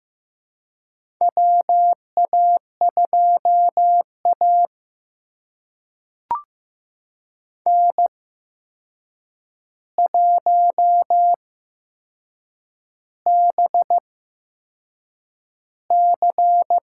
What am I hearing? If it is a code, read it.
Morse code, heard as a keyed tone, about 15 wpm, spaced as for 6 wpm, 700 Hz. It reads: WA2A N1BC
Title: USA Call Signs - 4 Characters - 8x Character Spacing - 15wpm